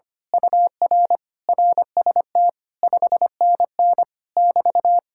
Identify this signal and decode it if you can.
Morse code, heard as a keyed tone, about 24 wpm, 700 Hz. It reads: UR RST 5NN <BT>
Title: Speed-Racing Common QSO Phrases - 25wpm